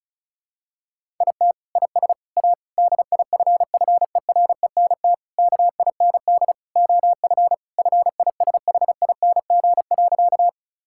Morse code, heard as a keyed tone, about 36 wpm, 700 Hz. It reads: IT IS A DIFFERENT KIND OF FISHING.
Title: Sentences from Top 1000 Words - Review Part 5 - 35wpm